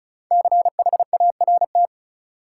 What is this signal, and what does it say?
Morse code, heard as a keyed tone, about 36 wpm, 700 Hz. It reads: CHART
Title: Sentences from Top 1300 Words - New Word Drill Alt - 35wpm